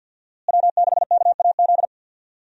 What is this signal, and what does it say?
Morse code, heard as a keyed tone, about 51 wpm, 700 Hz. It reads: W6KAZ
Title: USA Call Signs - 5 Characters - Repeated 3x - 50wpm